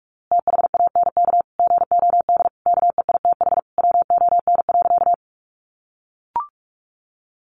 Morse code, heard as a keyed tone, about 44 wpm, 700 Hz. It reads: THANK GOD KEITH WON.
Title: A Sound Of Thunder 45wpm